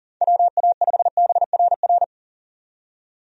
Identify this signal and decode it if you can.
Morse code, heard as a keyed tone, about 40 wpm, 700 Hz. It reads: WA5BRR
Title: USA Call Signs - 6 Characters - 40wpm